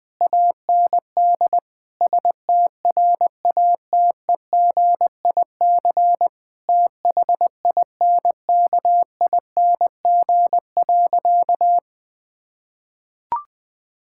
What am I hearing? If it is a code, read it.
Morse code, heard as a keyed tone, about 19 wpm, 700 Hz. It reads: AND STRATEGIC THINKING.
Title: Difficult Sentences with Hints - 20wpm